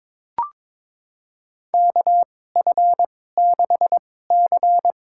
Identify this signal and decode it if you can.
Morse code, heard as a keyed tone, about 22 wpm, spaced as for 16 wpm, 700 Hz. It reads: KF6C
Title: USA Call Signs - 2x Character Spacing - 22wpm